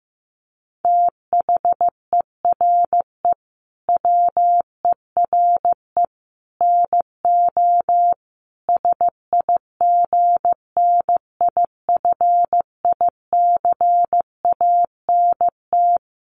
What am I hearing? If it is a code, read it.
Morse code, heard as a keyed tone, about 15 wpm, 700 Hz. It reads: THERE WERE NO SIGNIFICANT
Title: Sets of 5 Words Encore2 15wpm